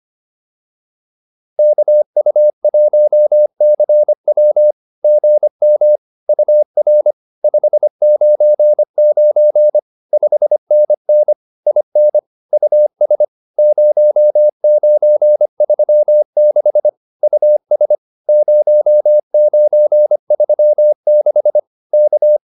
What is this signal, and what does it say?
Morse code, heard as a keyed tone, about 25 wpm, 600 Hz. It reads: KU1CW GM UR 599 5NN IN US 0936 US 0936 K